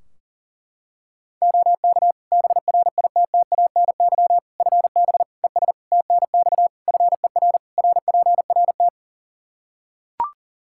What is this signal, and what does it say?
Morse code, heard as a keyed tone, about 40 wpm, 700 Hz. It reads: OK BRITTANY FB ES TNX FER RPRT
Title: K1USN SST QSO Practice 40wpm